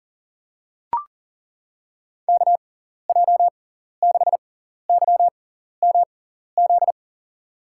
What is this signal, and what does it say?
Morse code, heard as a keyed tone, about 40 wpm, spaced as for 14 wpm, 700 Hz. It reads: KJ6YMZ
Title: USA Call Signs - 6 Characters - 6x Character Spacing - 40wpm